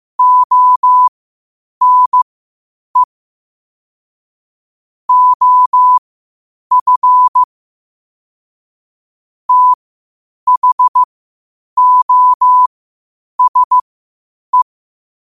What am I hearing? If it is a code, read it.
Morse code, heard as a keyed tone, about 15 wpm, spaced as for 8 wpm, 1000 Hz. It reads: ONE OF THOSE